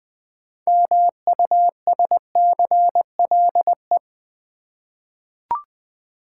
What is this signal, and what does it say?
Morse code, heard as a keyed tone, about 20 wpm, 700 Hz. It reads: MUSCLE
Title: 6 Letter Words - First 3 Letters Given - 20wpm